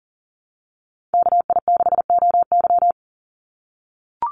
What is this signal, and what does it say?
Morse code, heard as a keyed tone, about 40 wpm, 700 Hz. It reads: KI6OY
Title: Call Signs - 40wpm